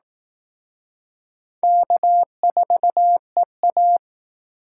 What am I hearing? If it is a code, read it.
Morse code, heard as a keyed tone, about 18 wpm, 700 Hz. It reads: K4EA
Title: Speed Racing - USA 1x2 Call Signs 15wpm